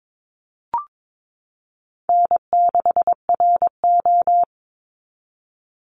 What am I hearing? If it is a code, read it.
Morse code, heard as a keyed tone, about 22 wpm, 700 Hz. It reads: N6RO
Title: Easy Call Signs Repeated 3x - 22wpm